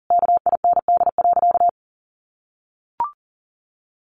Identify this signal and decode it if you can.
Morse code, heard as a keyed tone, about 40 wpm, 700 Hz. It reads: KIND.
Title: Sentences from Top 1500 Words 40wpm